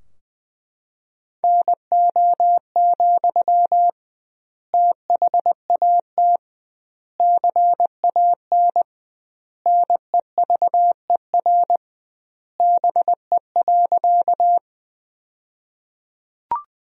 Morse code, heard as a keyed tone, about 20 wpm, 700 Hz. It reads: NO, THAT CAN NEVER BE.
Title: 2x Word-Spacing - Sentences from Top 300 Words - 20wpm